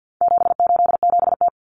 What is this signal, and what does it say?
Morse code, heard as a keyed tone, about 50 wpm, 700 Hz. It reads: 787T
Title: Common Cut Numbers - 4-Digit 50wpm